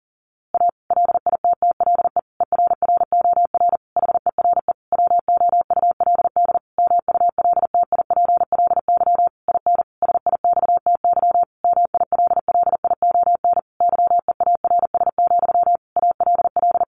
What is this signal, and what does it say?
Morse code, heard as a keyed tone, about 37 wpm, 700 Hz. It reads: A LITTLE ERROR HERE WOULD MULTIPLY IN SIXTY MILLION YEARS, ALL
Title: A Sound Of Thunder 40wpm